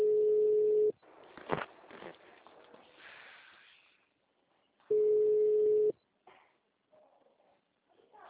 Ringback tone (Europe, 425 Hz).
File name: out- -2057- - - .1620